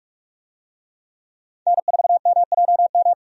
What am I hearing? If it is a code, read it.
Morse code, heard as a keyed tone, about 42 wpm, 700 Hz. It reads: N4MJM
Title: USA Call Signs - 5 Characters - 45wpm